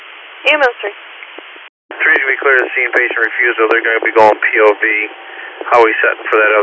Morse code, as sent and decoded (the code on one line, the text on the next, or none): none